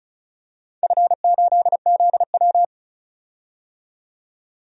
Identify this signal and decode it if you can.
Morse code, heard as a keyed tone, about 34 wpm, 700 Hz. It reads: F8ZW
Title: DXCC-100 Call Signs - 4 Characters - 35wpm